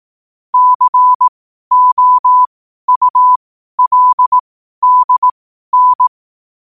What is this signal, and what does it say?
Morse code, heard as a keyed tone, about 18 wpm, spaced as for 12 wpm, 1000 Hz. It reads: COULDN
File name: qotdfiles / 12wpm